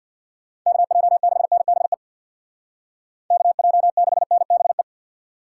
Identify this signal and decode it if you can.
Morse code, heard as a keyed tone, about 65 wpm, 700 Hz. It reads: KJ6NBE KJ6NBE